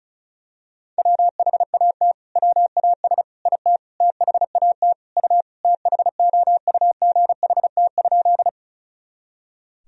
Morse code, heard as a keyed tone, about 36 wpm, 700 Hz. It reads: WHAT WAS IT THAT U THOUGHT?
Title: Sentences from Top 300 Words and Abbreviations 35wpm